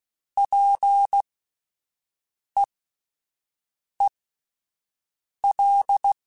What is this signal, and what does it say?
Morse code, heard as a keyed tone, about 16 wpm, spaced as for 5 wpm, 800 Hz. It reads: PEEL